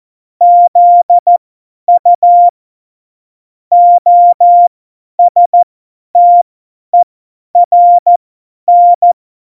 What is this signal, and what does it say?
Morse code, heard as a keyed tone, about 14 wpm, spaced as for 10 wpm, 700 Hz. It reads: ZU OSTERN
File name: Morsekurs / Witze / Witz 21 April 2020, 50 F70